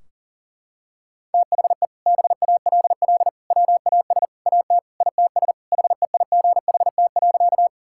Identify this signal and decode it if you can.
Morse code, heard as a keyed tone, about 40 wpm, 700 Hz. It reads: THE BALL WAS AT ITS HEIGHT.
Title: Sentences from Top 1600 Words 40wpm